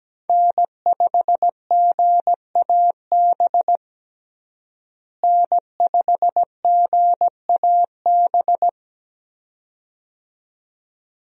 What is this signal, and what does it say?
Morse code, heard as a keyed tone, about 17 wpm, 700 Hz. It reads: N5GAB N5GAB
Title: USA Call Signs - 5 Characters - Repeated 2x - 17wpm